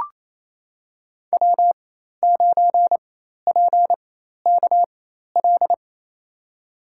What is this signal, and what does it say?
Morse code, heard as a keyed tone, about 28 wpm, spaced as for 13 wpm, 700 Hz. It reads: W9PKL
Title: USA Call Signs - 5 Characters - 4x Character Spacing - 28wpm